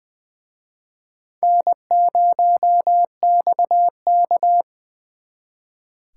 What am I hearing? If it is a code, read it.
Morse code, heard as a keyed tone, about 25 wpm, 700 Hz. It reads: N0XK
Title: Speed Racing - USA 1x2 Call Signs 20wpm